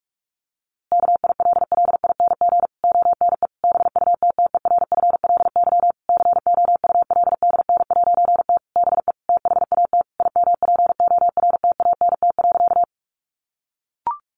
Morse code, heard as a keyed tone, about 44 wpm, 700 Hz. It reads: KILLING ONE BUTTERFLY COULDN'T BE THAT IMPORTANT.
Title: A Sound Of Thunder 45wpm